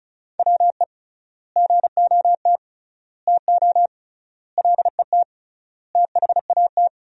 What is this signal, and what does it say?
Morse code, heard as a keyed tone, about 36 wpm, 700 Hz. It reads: WE GOT TO LET THAT
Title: 3x Word-Spacing - Sentences from Top 300 Words - 35wpm